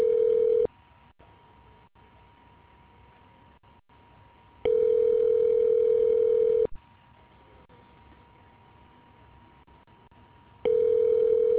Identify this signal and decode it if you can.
Ringback tone (North America), 440 + 480 Hz.